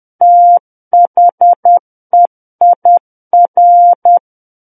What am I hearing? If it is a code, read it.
Morse code, heard as a keyed tone, about 10 wpm, 700 Hz. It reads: THEIR
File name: Book One / 01-Chapter 1 - 10 WPM